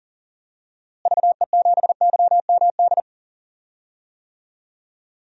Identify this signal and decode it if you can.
Morse code, heard as a keyed tone, about 40 wpm, 700 Hz. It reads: VE7YMD
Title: DXCC-100 Call Signs - 6 Characters - Repeated 2x - 40wpm